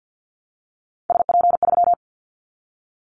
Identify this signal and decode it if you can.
Morse code, heard as a keyed tone, about 50 wpm, 700 Hz. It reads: SP3
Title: Callsign Prefixes for Top 25 DXCC - Repeated 3x - 50wpm